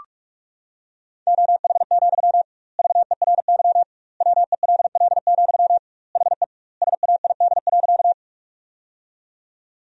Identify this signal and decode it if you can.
Morse code, heard as a keyed tone, about 46 wpm, 700 Hz. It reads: OH, VERY WELL, HE SAID.
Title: 2x Word-Spacing - Sentences from Top 500 Words - 45wpm